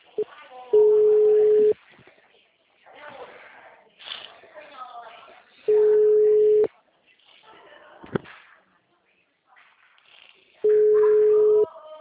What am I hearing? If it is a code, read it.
Ringback tone (Europe, 425 Hz), with speech.
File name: out- -2018- - - .319